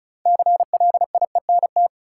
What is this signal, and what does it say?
Morse code, heard as a keyed tone, about 36 wpm, 700 Hz. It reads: CLIENT
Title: Six Letter Words - 35wpm